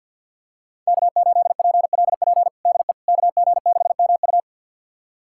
Morse code, heard as a keyed tone, about 49 wpm, 700 Hz. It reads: K8PLP DE KG6MU
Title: USA Callsign DE Callsign - 5 Characters - 50wpm